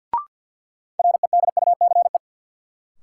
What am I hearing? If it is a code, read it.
Morse code, heard as a keyed tone, about 52 wpm, 700 Hz. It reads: REDUCE
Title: Sentences from Top 1100 Words - New Word Drill Alt - 50wpm